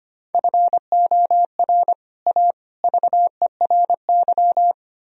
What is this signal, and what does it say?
Morse code, heard as a keyed tone, about 32 wpm, 700 Hz. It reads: FOR A VERY